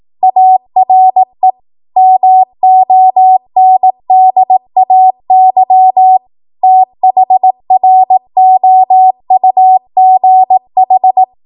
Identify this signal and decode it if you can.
Morse code, heard as a keyed tone, about 18 wpm, 750 Hz. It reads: ARE MONDAY THROUGH
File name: Archive / 18 WPM / 18WPM